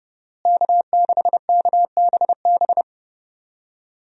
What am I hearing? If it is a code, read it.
Morse code, heard as a keyed tone, about 31 wpm, 700 Hz. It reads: K6KBB
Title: USA Call Signs - 5 Characters - Repeated 3x - 30wpm